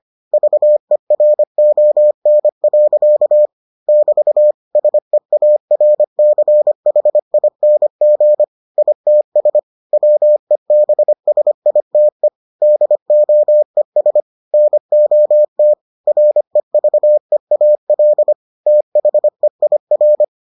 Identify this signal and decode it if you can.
Morse code, heard as a keyed tone, about 25 wpm, 600 Hz. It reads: VERON. <BT> SEARCHING ITS WEBSITE DOES NOT REVEAL THEIR